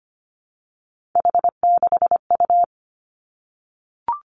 Morse code, heard as a keyed tone, about 25 wpm, 700 Hz. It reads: H6U
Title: Three Challenging Letter-Number Combinations 25wpm